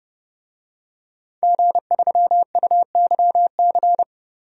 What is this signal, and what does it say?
Morse code, heard as a keyed tone, about 30 wpm, 700 Hz. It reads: G3UYC
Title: DXCC-100 Call Signs - 5 Characters - Repeated 3x - 30wpm